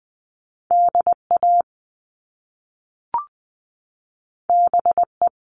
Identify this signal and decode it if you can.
Morse code, heard as a keyed tone, about 20 wpm, 700 Hz. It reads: DA BE
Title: Top Two Letter Combinations in Words 20wpm